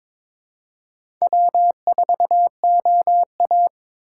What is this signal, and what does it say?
Morse code, heard as a keyed tone, about 22 wpm, 700 Hz. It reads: W4OA
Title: USA Call Signs - 2x Character Spacing - 22wpm